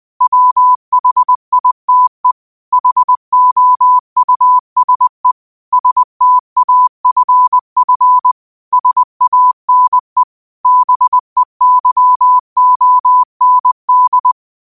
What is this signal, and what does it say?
Morse code, heard as a keyed tone, about 20 wpm, 1000 Hz. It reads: WHITE HOUSE STAFF SANE BEYOND
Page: New quotes every day in morse code at 20 Words per minute.